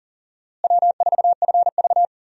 Morse code, heard as a keyed tone, about 38 wpm, 700 Hz. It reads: W4FV
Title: USA Call Signs - 4 Characters - Repeated 2x - 40wpm